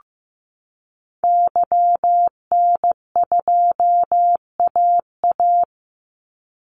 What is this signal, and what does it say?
Morse code, heard as a keyed tone, about 15 wpm, 700 Hz. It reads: YN2AA
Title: Call Signs Repeated 3x - 15wpm